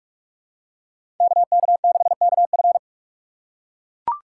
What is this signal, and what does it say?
Morse code, heard as a keyed tone, about 43 wpm, 700 Hz. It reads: KK6KF
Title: USA Call Signs - 5 Characters - 2x Character Spacing - 45wpm